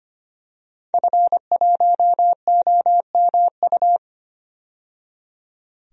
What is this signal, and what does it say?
Morse code, heard as a keyed tone, about 25 wpm, 700 Hz. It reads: F1OMU
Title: DXCC-100 Call Signs - 5 Characters - 25wpm